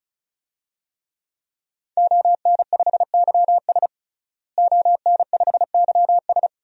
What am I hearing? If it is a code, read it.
Morse code, heard as a keyed tone, about 35 wpm, 700 Hz. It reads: ON5YS ON5YS
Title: DXCC-100 Call Signs - 5 Characters - Repeated 3x - 35wpm